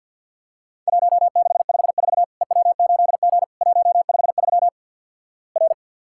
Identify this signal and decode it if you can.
Morse code, heard as a keyed tone, about 50 wpm, 700 Hz. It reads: 1654 EW8G 153 R
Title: Worked All Europe DX Contest - QTC Exchange with Repeats - 50wpm